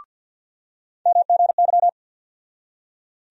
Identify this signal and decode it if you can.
Morse code, heard as a keyed tone, about 46 wpm, 700 Hz. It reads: MGY
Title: USA Call Sign Postfixes - 3 Letters 50wpm